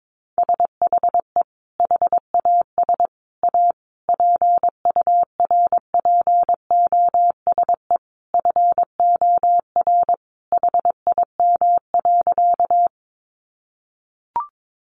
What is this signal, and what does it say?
Morse code, heard as a keyed tone, about 22 wpm, 700 Hz. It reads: SHE HAS A PURPOSE FOR HIM.
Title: Sentences from Top 800 Words 22wpm